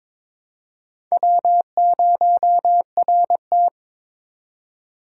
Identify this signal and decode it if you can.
Morse code, heard as a keyed tone, about 22 wpm, 700 Hz. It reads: W0RT
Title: USA Call Signs - 4 Characters - 4x Character Spacing - 22wpm